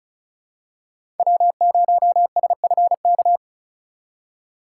Morse code, heard as a keyed tone, about 34 wpm, 700 Hz. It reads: W0SFK
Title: USA Call Signs - 4x Character Spacing - 35wpm